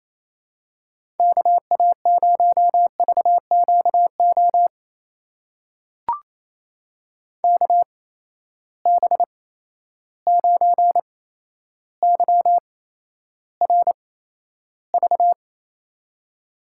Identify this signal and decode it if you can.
Morse code, heard as a keyed tone, about 28 wpm, spaced as for 12 wpm, 700 Hz. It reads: KA0VQO KB9YRV
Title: USA Call Signs - 6 Characters - 8x Character Spacing - 28wpm